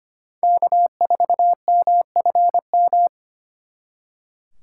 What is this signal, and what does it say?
Morse code, heard as a keyed tone, about 32 wpm, 700 Hz. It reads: K4MFM